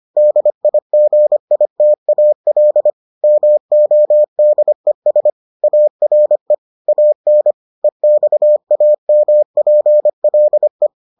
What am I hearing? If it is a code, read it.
Morse code, heard as a keyed tone, about 25 wpm, 600 Hz. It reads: DIGITAL MODES ARE AN EXAMPLE